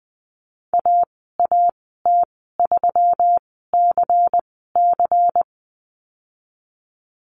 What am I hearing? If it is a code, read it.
Morse code, heard as a keyed tone, about 19 wpm, spaced as for 14 wpm, 700 Hz. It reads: AAT3CC
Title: 128 Lesson 21g - Number 6 - Call Signs 20wpm